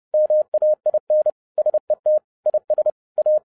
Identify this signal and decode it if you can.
Morse code, heard as a keyed tone, about 30 wpm, 610 Hz. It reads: MAIN SET IS A